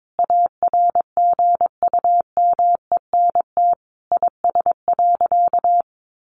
Morse code, heard as a keyed tone, about 22 wpm, 700 Hz. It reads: ARGUMENT IS.
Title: Sentences from Top 1400 Words 22wpm